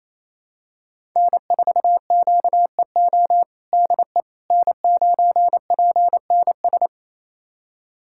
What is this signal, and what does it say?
Morse code, heard as a keyed tone, about 28 wpm, 700 Hz. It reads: N4QEO DE N9PNS
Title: USA Callsign DE Callsign - 5 Characters - 28wpm